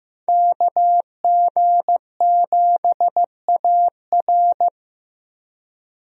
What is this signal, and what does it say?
Morse code, heard as a keyed tone, about 15 wpm, 700 Hz. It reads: KG7AR
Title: USA Call Signs - 5 Characters - Repeated 3x - 15wpm